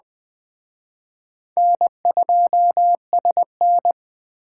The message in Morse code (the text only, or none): N2SN